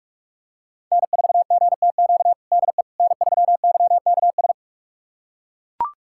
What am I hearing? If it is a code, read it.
Morse code, heard as a keyed tone, about 42 wpm, 700 Hz. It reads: N4GTQ DE N3YKS
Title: USA Callsign DE Callsign - 5 Characters - 45wpm